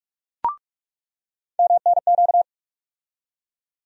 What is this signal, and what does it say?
Morse code, heard as a keyed tone, about 43 wpm, 700 Hz. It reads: MNQ
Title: USA Call Sign Postfixes - 3 Letters 45wpm